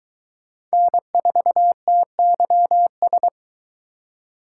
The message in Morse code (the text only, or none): N4TYS